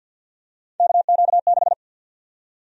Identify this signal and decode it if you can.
Morse code, heard as a keyed tone, about 50 wpm, 700 Hz. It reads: KQ6
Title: USA Call Sign Prefixes - 50wpm